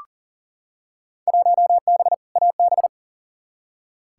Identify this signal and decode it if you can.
Morse code, heard as a keyed tone, about 40 wpm, 700 Hz. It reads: 1B AB
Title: Field Day Exchanges - 40wpm